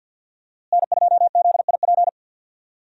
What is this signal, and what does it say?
Morse code, heard as a keyed tone, about 49 wpm, 700 Hz. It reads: N2ZIP
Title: USA Call Signs - 5 Characters - Repeated 3x - 50wpm